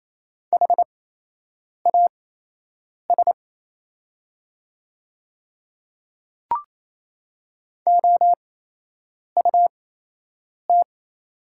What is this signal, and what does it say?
Morse code, heard as a keyed tone, about 28 wpm, spaced as for 8 wpm, 700 Hz. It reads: HAS OUT
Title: 8x Character Spacing - Top 200 Words - 28wpm